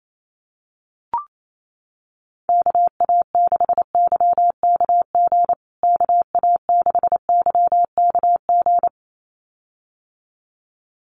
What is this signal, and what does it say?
Morse code, heard as a keyed tone, about 28 wpm, 700 Hz. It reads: KA6YKG KA6YKG
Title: USA Call Signs Repeated 2x 28wpm